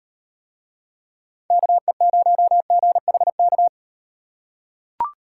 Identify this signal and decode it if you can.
Morse code, heard as a keyed tone, about 38 wpm, 700 Hz. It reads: KE0GHK